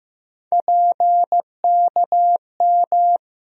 Morse code, heard as a keyed tone, about 15 wpm, 700 Hz. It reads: PKM
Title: USA Call Sign Postfixes - 3 Letters 15wpm